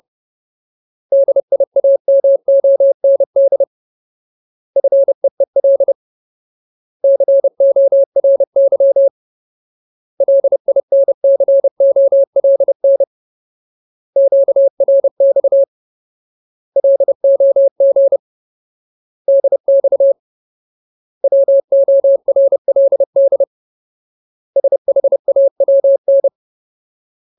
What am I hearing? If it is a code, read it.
Morse code, heard as a keyed tone, about 30 wpm, 550 Hz. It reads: DIAMOND FEEL CORY LINCOLN QRX LOG DX WORLD SHAWN
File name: Ultimate Word List / 30WPM Ultimate